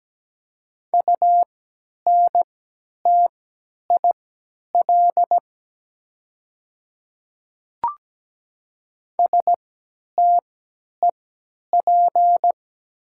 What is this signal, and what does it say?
Morse code, heard as a keyed tone, about 17 wpm, spaced as for 10 wpm, 700 Hz. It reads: UNTIL STEP
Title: 3x Character Spacing - Top 500 Words - 17wpm